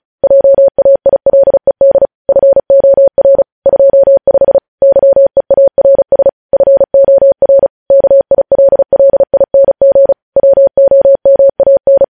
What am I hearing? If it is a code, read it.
Morse code, heard as a keyed tone, about 36 wpm, 550 Hz. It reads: JAILED FOR 25 YEARS FOR KILLING WOMAN
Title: News Headlines at 35